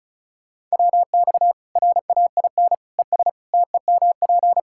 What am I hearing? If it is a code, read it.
Morse code, heard as a keyed tone, about 36 wpm, 700 Hz. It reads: WX RAIN ES TEMP
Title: Practice QSO - Exchange 2 - Info - 35wpm